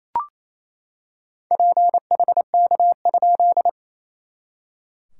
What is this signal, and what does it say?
Morse code, heard as a keyed tone, about 28 wpm, 690 Hz. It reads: PHK?
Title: SOTA QSO Practice 28wpm